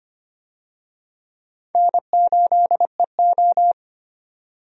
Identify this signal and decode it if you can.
Morse code, heard as a keyed tone, about 25 wpm, 700 Hz. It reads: N8EO
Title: USA Call Signs - 4 Characters - 25wpm